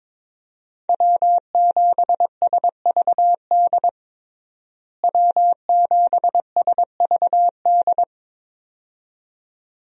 Morse code, heard as a keyed tone, about 22 wpm, 700 Hz. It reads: W7SVD W7SVD
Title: USA Call Signs - 5 Characters - Repeated 2x - 22wpm